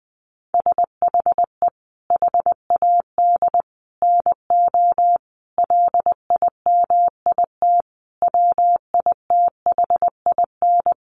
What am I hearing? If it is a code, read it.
Morse code, heard as a keyed tone, about 20 wpm, 700 Hz. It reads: SHE HAD NO LIMIT WITHIN
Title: Sentences from Top 1500 Words 20wpm